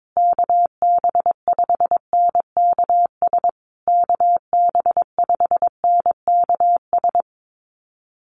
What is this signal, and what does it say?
Morse code, heard as a keyed tone, about 22 wpm, 700 Hz. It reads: KB5NKS KB5NKS
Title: USA Call Signs Repeated 3x 22wpm